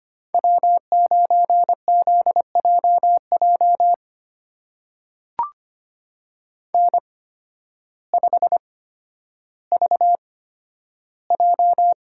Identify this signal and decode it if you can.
Morse code, heard as a keyed tone, about 25 wpm, spaced as for 12 wpm, 700 Hz. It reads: W9ZJJ N5VJ
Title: USA Call Signs - 5 Characters - 8x Character Spacing - 25wpm